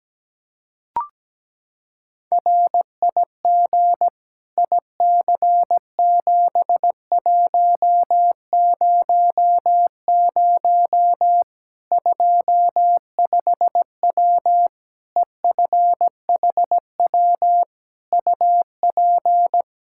Morse code, heard as a keyed tone, about 17 wpm, 700 Hz. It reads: RIG IC7100 25W EFHW UP
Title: Practice QSO - Exchange 2 - Info - 17wpm